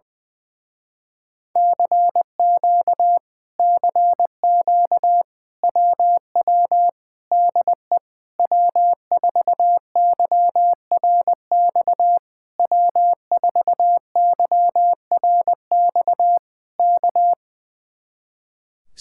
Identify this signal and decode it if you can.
Morse code, heard as a keyed tone, about 20 wpm, 700 Hz. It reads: CQ CQ WW DE W4YRX W4YRX K
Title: Speed-Racing Common QSO Phrases - 20wpm